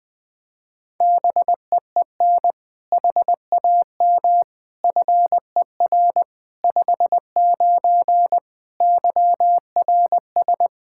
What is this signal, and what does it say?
Morse code, heard as a keyed tone, about 20 wpm, 700 Hz. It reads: BEEN HAM FER 59 YRS
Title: Practice QSO - Exchange 3 - More Info - 20wpm